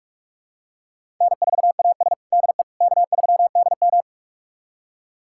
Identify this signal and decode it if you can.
Morse code, heard as a keyed tone, about 41 wpm, 700 Hz. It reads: N4AS DE K3DM
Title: USA Callsign DE Callsign - 4 Characters - 45wpm